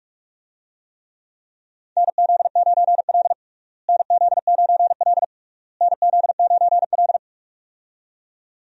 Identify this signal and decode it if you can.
Morse code, heard as a keyed tone, about 46 wpm, 700 Hz. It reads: NZ9L NZ9L NZ9L
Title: USA Call Signs - 4 Characters - Repeated 3x - 45wpm